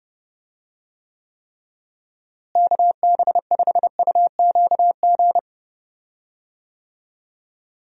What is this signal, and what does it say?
Morse code, heard as a keyed tone, about 30 wpm, 700 Hz. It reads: KB5UQG
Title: USA Call Signs - 6 Characters - Repeated 3x - 30wpm